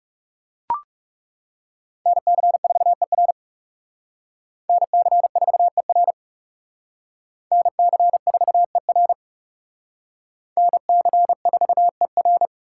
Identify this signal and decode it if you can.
Morse code, heard as a keyed tone, about 38 wpm, 700 Hz. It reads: NC4ER NC4ER NC4ER NC4ER